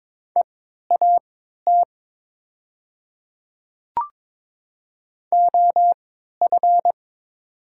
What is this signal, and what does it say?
Morse code, heard as a keyed tone, about 22 wpm, spaced as for 12 wpm, 700 Hz. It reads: EAT OF
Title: 3x Character Spacing - Top 500 Words - 22wpm